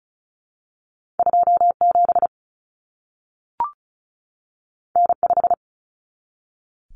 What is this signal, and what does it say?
Morse code, heard as a keyed tone, about 36 wpm, 700 Hz. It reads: 27 N5
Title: Common Cut Numbers - 2-Digit 35wpm